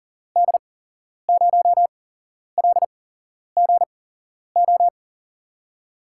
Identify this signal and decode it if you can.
Morse code, heard as a keyed tone, about 40 wpm, spaced as for 11 wpm, 700 Hz. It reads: D0LGO
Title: USA Call Signs - 6 Characters - 8x Character Spacing - 40wpm